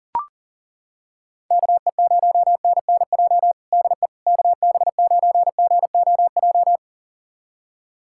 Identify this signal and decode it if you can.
Morse code, heard as a keyed tone, about 40 wpm, 700 Hz. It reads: KE0NNJ DE KB9GOJ
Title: USA Callsign DE Callsign - 6 Characters - 40wpm